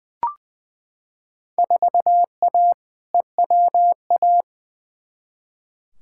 Morse code, heard as a keyed tone, about 20 wpm, 700 Hz. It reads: 4A EWA
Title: Field Day Exchanges - 20wpm